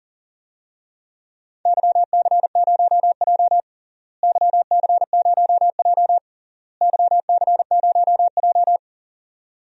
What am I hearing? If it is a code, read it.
Morse code, heard as a keyed tone, about 40 wpm, 700 Hz. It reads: YC0J YC0J YC0J
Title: DXCC-100 Call Signs - 4 Characters - Repeated 3x - 40wpm